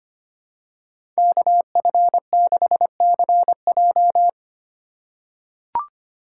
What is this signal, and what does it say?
Morse code, heard as a keyed tone, about 25 wpm, 700 Hz. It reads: KF6CJ
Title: USA Call Signs - 5 Characters - 3x Character Spacing - 25wpm